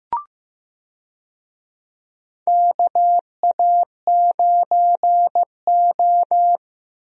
Morse code, heard as a keyed tone, about 15 wpm, 700 Hz. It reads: KA9O
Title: USA Call Signs - 4 Characters - Repeated 2x - 15wpm